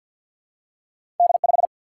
Morse code, heard as a keyed tone, about 53 wpm, 700 Hz. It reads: D 5
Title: Single-Letter Single-Number 50wpm